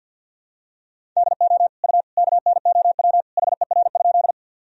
Morse code, heard as a keyed tone, about 52 wpm, 700 Hz. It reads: DO U KNOW HER?
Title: Sentences from Top 100 Words and Abbreviations 50wpm